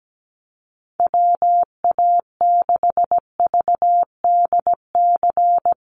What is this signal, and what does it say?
Morse code, heard as a keyed tone, about 17 wpm, 700 Hz. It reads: WA6VDC
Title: USA Call Signs Repeated 2x 17wpm